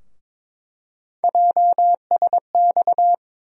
Morse code, heard as a keyed tone, about 22 wpm, 700 Hz. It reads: JSX
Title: USA Call Sign Postfixes - 3 Letters 22wpm